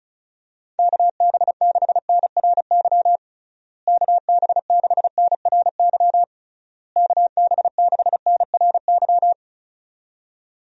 Morse code, heard as a keyed tone, about 36 wpm, 700 Hz. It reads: KB6NRY KB6NRY KB6NRY
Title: USA Call Signs - 6 Characters - Repeated 3x - 35wpm